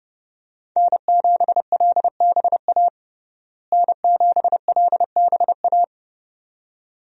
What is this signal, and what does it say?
Morse code, heard as a keyed tone, about 30 wpm, 700 Hz. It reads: N7LBA N7LBA
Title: USA Call Signs - 5 Characters - Repeated 2x - 30wpm